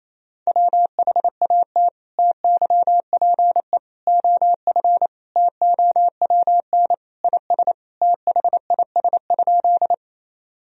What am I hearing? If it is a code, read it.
Morse code, heard as a keyed tone, about 28 wpm, 700 Hz. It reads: WHAT TYPE OF TOWN IS THIS?
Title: Sentences from Top 1000 Words - Review Part 3 - 28wpm